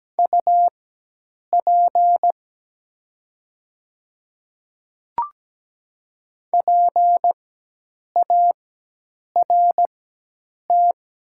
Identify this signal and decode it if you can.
Morse code, heard as a keyed tone, about 17 wpm, spaced as for 8 wpm, 700 Hz. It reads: UP PART
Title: 4x Character Spacing - Top 200 Words - 17wpm